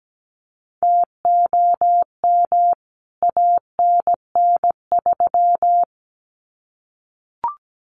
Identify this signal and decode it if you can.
Morse code, heard as a keyed tone, about 17 wpm, 700 Hz. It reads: TOM ANN3
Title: CWT Exchange 17wpm